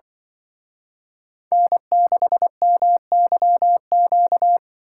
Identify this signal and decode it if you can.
Morse code, heard as a keyed tone, about 24 wpm, 700 Hz. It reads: N6MYQ
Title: Speed Racing - USA Call Signs 20wpm